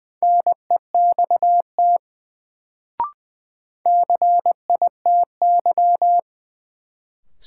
Morse code, heard as a keyed tone, about 20 wpm, 700 Hz. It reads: NEXT CITY
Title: Sentences from Top 300 Words - New Word Drill - 20wpm